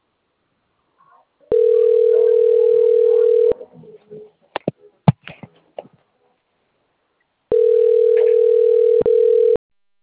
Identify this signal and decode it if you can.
Ringback tone (North America), 440 + 480 Hz.
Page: Duque de Caxias